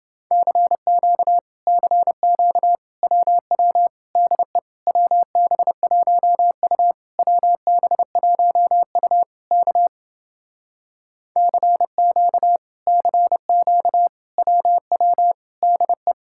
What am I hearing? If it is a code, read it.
Morse code, heard as a keyed tone, about 29 wpm, 700 Hz. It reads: CQ CQ WW DE WB1U WB1U K CQ CQ WW DE